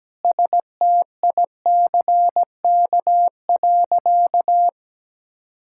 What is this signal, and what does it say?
Morse code, heard as a keyed tone, about 17 wpm, 700 Hz. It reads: STICK.
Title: Sentences from Top 1600 Words 17wpm